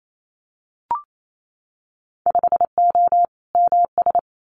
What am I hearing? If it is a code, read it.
Morse code, heard as a keyed tone, about 28 wpm, 700 Hz. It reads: 5O MS
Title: Winter Field Day Exchanges Encore 28wpm